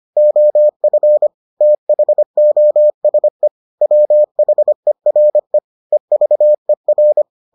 Morse code, heard as a keyed tone, about 25 wpm, 600 Hz. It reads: OF THOSE WHERE EVER